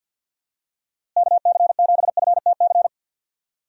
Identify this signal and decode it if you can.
Morse code, heard as a keyed tone, about 51 wpm, 700 Hz. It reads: KC7FTC
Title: USA Call Signs - 6 Characters - Repeated 3x - 50wpm